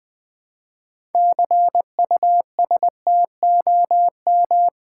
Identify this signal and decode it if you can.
Morse code, heard as a keyed tone, about 20 wpm, 700 Hz. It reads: CUSTOM
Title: Build A Word - 2 Letters At A Time - 8-Letter Words - 20wpm